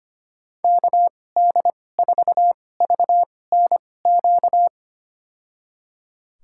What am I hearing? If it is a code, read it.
Morse code, heard as a keyed tone, about 25 wpm, spaced as for 18 wpm, 700 Hz. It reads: KD4VNQ
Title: USA Call Signs - 2x Character Spacing - 25wpm